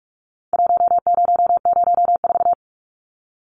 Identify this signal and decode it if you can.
Morse code, heard as a keyed tone, about 44 wpm, 700 Hz. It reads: 1004
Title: Numbers - 4-Digit 45wpm